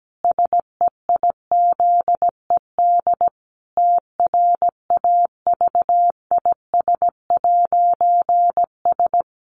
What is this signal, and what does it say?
Morse code, heard as a keyed tone, about 17 wpm, 700 Hz. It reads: SEIZED TRAVIS'S
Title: A Sound Of Thunder 17wpm